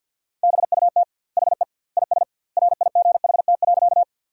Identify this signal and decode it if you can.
Morse code, heard as a keyed tone, about 50 wpm, 700 Hz. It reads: BUT HE IS RIGHT.
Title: 2x Word-Spacing - Sentences from Top 200 Words - 50wpm